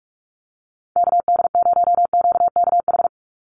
Morse code, heard as a keyed tone, about 44 wpm, 700 Hz. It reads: KD0QKH
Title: USA Call Signs 45wpm